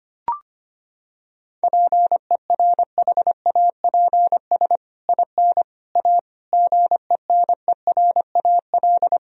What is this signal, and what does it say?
Morse code, heard as a keyed tone, about 25 wpm, 700 Hz. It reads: PERHAPS IN A GENERAL
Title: Sentences from Top 1000 Words - Review Part 3 - 25wpm